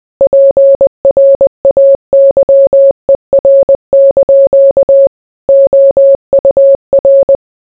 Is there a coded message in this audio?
Yes: Morse code from 0.2 to 7.4 s, heard as a keyed tone, about 20 wpm, 550 Hz. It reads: PRAYER) OUR